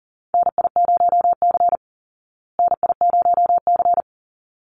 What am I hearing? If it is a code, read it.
Morse code, heard as a keyed tone, about 40 wpm, 700 Hz. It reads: NI0C NI0C
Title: Call Signs Repeated 3x - 40wpm